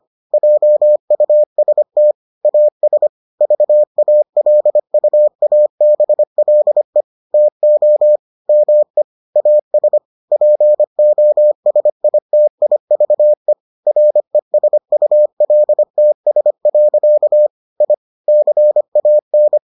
Morse code, heard as a keyed tone, about 25 wpm, 600 Hz. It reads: JUST AS VALUABLE TO ME AS POSITIVE RESULTS. I CAN